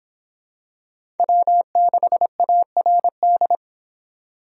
Morse code, heard as a keyed tone, about 25 wpm, 700 Hz. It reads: W6ARD